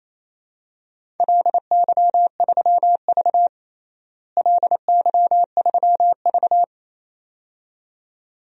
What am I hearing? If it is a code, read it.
Morse code, heard as a keyed tone, about 28 wpm, 700 Hz. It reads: LY3V LY3V
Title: DXCC-100 Call Signs - 4 Characters - Repeated 2x - 28wpm